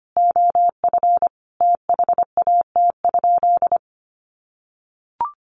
Morse code, heard as a keyed tone, about 25 wpm, 700 Hz. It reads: OF THAT?
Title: Sentences from Top 200 Words 25wpm